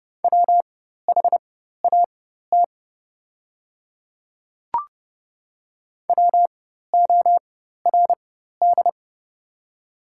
Morse code, heard as a keyed tone, about 30 wpm, spaced as for 14 wpm, 700 Hz. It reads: WHAT WORD
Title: 4x Character Spacing - Top 100 Words - 30wpm